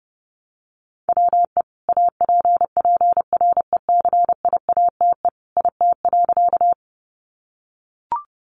Morse code, heard as a keyed tone, about 30 wpm, 700 Hz. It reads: WE APPRECIATE IT.
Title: Sentences from Top 1300 Words 30wpm